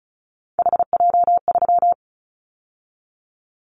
Morse code, heard as a keyed tone, about 36 wpm, 700 Hz. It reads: HJ3
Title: Callsign Prefixes for Top 100 DXCC - 35wpm